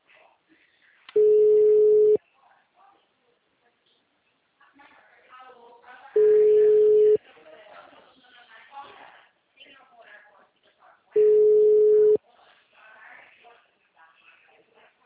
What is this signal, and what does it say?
Ringback tone (Europe, 425 Hz), with speech.